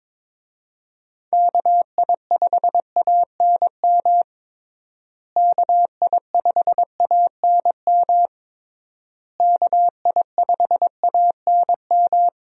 Morse code, heard as a keyed tone, about 22 wpm, 700 Hz. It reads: KI5ANM KI5ANM KI5ANM
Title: USA Call Signs - 6 Characters - Repeated 3x - 22wpm